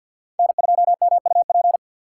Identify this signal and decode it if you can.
Morse code, heard as a keyed tone, about 64 wpm, 700 Hz. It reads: N2MUP